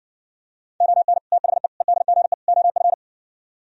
Morse code, heard as a keyed tone, about 60 wpm, 700 Hz. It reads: ON THE EDGE OF